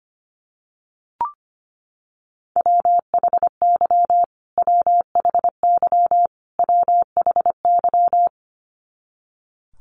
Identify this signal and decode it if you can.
Morse code, heard as a keyed tone, about 25 wpm, 700 Hz. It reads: WHY WHY WHY
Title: Top 300 Words Repeated 25wpm